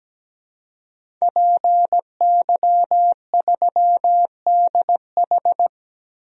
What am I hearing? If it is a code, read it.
Morse code, heard as a keyed tone, about 17 wpm, 700 Hz. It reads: PY3DH
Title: DXCC-100 Call Signs - 5 Characters - 17wpm